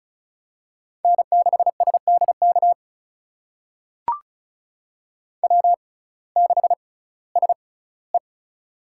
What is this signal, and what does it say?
Morse code, heard as a keyed tone, about 36 wpm, spaced as for 19 wpm, 700 Hz. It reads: N6SDK W6SE
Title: USA Call Signs - 6x Character Spacing - 35wpm